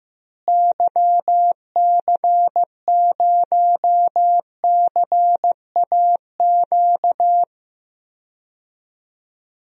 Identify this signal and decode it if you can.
Morse code, heard as a keyed tone, about 15 wpm, 700 Hz. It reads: YC0CAQ
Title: DXCC-100 Call Signs - 6 Characters - Repeated 2x - 15wpm